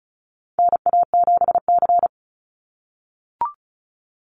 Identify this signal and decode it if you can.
Morse code, heard as a keyed tone, about 34 wpm, 700 Hz. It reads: NA7C
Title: Call Signs - 35wpm